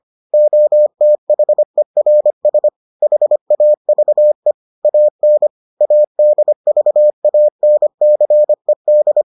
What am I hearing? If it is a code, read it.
Morse code, heard as a keyed tone, about 25 wpm, 600 Hz. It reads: OTHERS HAVE AN ADVANCED